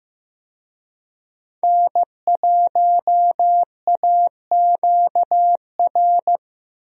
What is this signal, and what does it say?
Morse code, heard as a keyed tone, about 15 wpm, 700 Hz. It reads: N1AQR
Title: USA Call Signs - 5 Characters - 15wpm